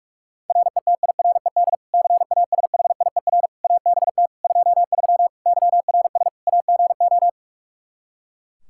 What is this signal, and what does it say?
Morse code, heard as a keyed tone, about 47 wpm, 700 Hz. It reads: RETIRED CASHIER ABT 23 YRS AGO
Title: Practice QSO - Exchange 3 - More Info - 45wpm